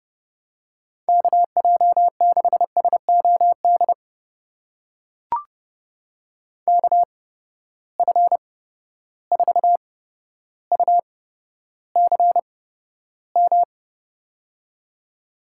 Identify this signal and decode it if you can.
Morse code, heard as a keyed tone, about 30 wpm, spaced as for 13 wpm, 700 Hz. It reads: KJ6SOD KF4UCM
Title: USA Call Signs - 6 Characters - 8x Character Spacing - 30wpm